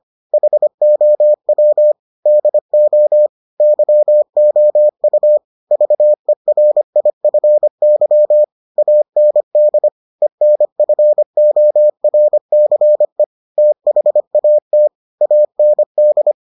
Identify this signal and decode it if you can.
Morse code, heard as a keyed tone, about 25 wpm, 600 Hz. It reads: HOW DO YOU VERIFY AND ENFORCE THAT AND